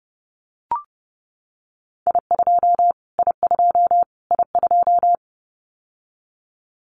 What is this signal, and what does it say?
Morse code, heard as a keyed tone, about 30 wpm, 700 Hz. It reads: I2 I2 I2
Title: Callsign Prefixes for Top 25 DXCC - Repeated 3x - 30wpm